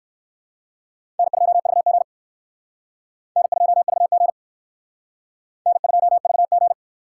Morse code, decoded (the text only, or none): N2VG N2VG N2VG